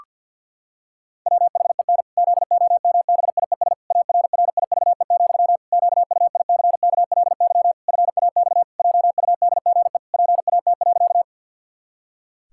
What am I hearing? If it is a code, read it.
Morse code, heard as a keyed tone, about 50 wpm, 700 Hz. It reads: WHEN ZOMBIES ARRIVE, QUICKLY FAX JUDGE PAT.
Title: Pangram Challenge v2 - 50wpm